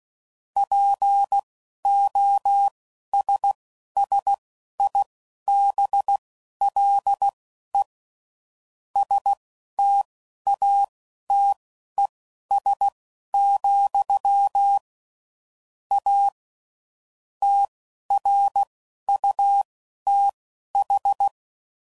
Morse code, heard as a keyed tone, about 16 wpm, spaced as for 11 wpm, 800 Hz. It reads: POSSIBLE STATES, A TRUTH
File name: CW Code Groups @10WPM / Technical Subjects / AND Gate